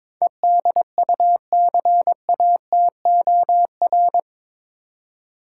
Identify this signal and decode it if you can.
Morse code, heard as a keyed tone, about 22 wpm, 700 Hz. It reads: EDUCATOR
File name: practice / 22wpm / Eight Letter Words - 22wpm